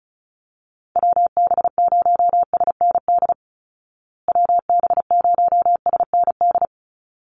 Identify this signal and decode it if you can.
Morse code, heard as a keyed tone, about 34 wpm, 700 Hz. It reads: WB0SND WB0SND
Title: Call Signs Repeated 3x - 35wpm